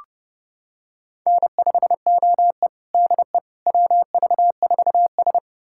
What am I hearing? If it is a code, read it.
Morse code, heard as a keyed tone, about 30 wpm, 700 Hz. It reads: N5OE DE WV4S
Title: USA Callsign DE Callsign - 4 Characters - 30wpm